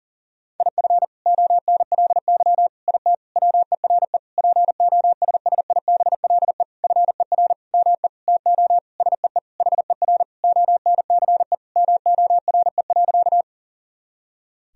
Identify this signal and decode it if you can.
Morse code, heard as a keyed tone, about 40 wpm, 700 Hz. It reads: IF ONLY IT WERE POSSIBLE FER ME TO SEE HER ONCE MORE.
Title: Sentences from Top 500 Words and Abbreviations 40wpm